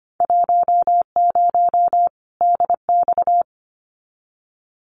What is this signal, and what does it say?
Morse code, heard as a keyed tone, about 25 wpm, 700 Hz. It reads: 10 DX
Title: Winter Field Day Exchanges 25wpm